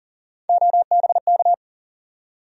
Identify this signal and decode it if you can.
Morse code, heard as a keyed tone, about 40 wpm, 700 Hz. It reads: OBK
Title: USA Call Sign Postfixes - 3 Letters 40wpm